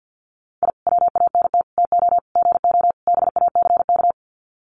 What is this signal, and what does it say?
Morse code, heard as a keyed tone, about 50 wpm, 700 Hz. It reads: I WANT TO GO BACK
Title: Sets of 5 Words Encore 50wpm